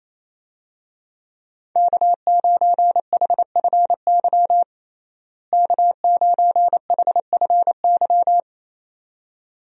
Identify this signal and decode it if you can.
Morse code, heard as a keyed tone, about 28 wpm, 700 Hz. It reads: K9HFY K9HFY
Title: USA Call Signs - 5 Characters - Repeated 2x - 28wpm